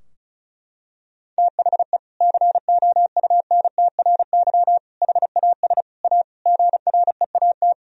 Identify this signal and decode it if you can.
Morse code, heard as a keyed tone, about 36 wpm, 700 Hz. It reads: THE COUNTRY HAS A GREAT
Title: Sentences from Top 1000 Words - Review Part 4 - 35wpm